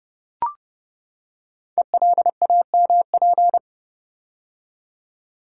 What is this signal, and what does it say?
Morse code, heard as a keyed tone, about 30 wpm, 700 Hz. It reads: ELAMP
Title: Code Groups - 5 Letters - 30wpm